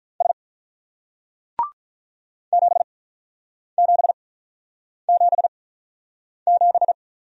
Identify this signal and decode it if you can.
Morse code, heard as a keyed tone, about 42 wpm, spaced as for 8 wpm, 700 Hz. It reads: S 7777
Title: Speed-Racing Single Letter-Number 35wpm